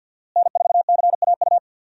Morse code, heard as a keyed tone, about 48 wpm, 700 Hz. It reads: N4CAU
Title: USA Call Signs - 5 Characters - Repeated 2x - 50wpm